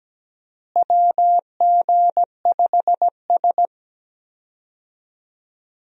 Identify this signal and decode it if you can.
Morse code, heard as a keyed tone, about 17 wpm, 700 Hz. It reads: WG5S
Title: USA Call Signs - 4 Characters - Repeated 2x - 17wpm